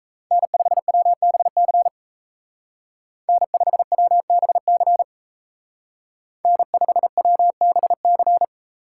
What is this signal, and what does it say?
Morse code, heard as a keyed tone, about 34 wpm, 700 Hz. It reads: N5WBC N5WBC N5WBC
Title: Speed Racing - USA 1x3 Call Signs 28wpm